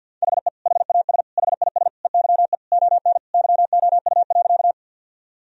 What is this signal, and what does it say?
Morse code, heard as a keyed tone, about 46 wpm, 700 Hz. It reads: HE HAS HIS EYE ON YOU.
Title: Sentences from Top 1000 Words - Review Part 1 - 50wpm